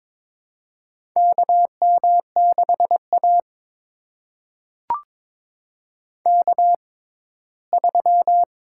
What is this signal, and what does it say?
Morse code, heard as a keyed tone, about 22 wpm, spaced as for 15 wpm, 700 Hz. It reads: KM6A K3
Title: USA Call Signs - 6x Character Spacing - 22wpm